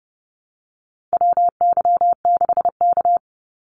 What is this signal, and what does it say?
Morse code, heard as a keyed tone, about 30 wpm, 700 Hz. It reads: WY6K
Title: Call Signs - 30wpm